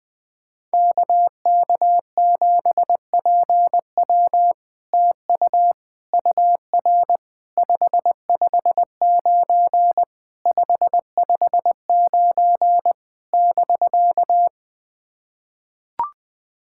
Morse code, heard as a keyed tone, about 20 wpm, 700 Hz. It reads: KK7PW TU UR 559 559 <BK>
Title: SOTA QSO Practice 20wpm